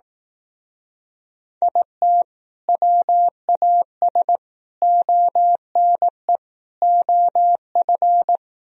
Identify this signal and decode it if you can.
Morse code, heard as a keyed tone, about 18 wpm, 700 Hz. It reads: IT WAS ONE OF
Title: Speed-Racing - Sets of 4 Words 15wpm